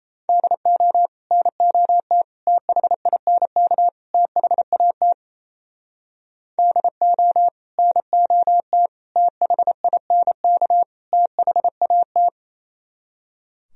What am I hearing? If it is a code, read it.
Morse code, heard as a keyed tone, about 36 wpm, 700 Hz. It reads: DO NOT THINK THAT DO NOT THINK THAT